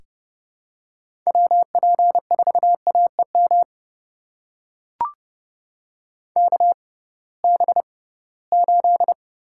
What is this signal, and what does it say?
Morse code, heard as a keyed tone, about 30 wpm, spaced as for 19 wpm, 700 Hz. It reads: WP4AEM KB8
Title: USA Call Signs - 6 Characters - 6x Character Spacing - 30wpm